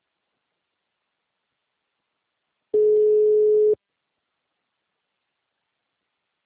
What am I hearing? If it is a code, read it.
Ringback tone (Europe, 425 Hz).
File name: out- -2012- - - .43510